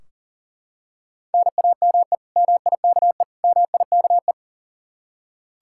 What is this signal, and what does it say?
Morse code, heard as a keyed tone, about 40 wpm, 700 Hz. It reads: NAME MIKE MIKE
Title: Practice QSO - Exchange 1 - Beginning - 40wpm